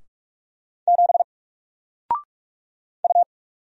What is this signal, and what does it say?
Morse code, heard as a keyed tone, about 46 wpm, 700 Hz. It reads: 7 U
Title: Single Letter-Number Alt 45wpm